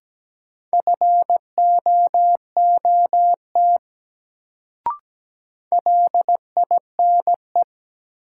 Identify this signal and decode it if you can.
Morse code, heard as a keyed tone, about 17 wpm, 700 Hz. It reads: FOOT LINE
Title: Sets of 5 Words - New Word Drill - 17wpm